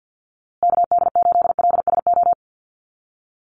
Morse code, heard as a keyed tone, about 50 wpm, 700 Hz. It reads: KD8LSO
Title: USA Call Signs Repeated 2x 50wpm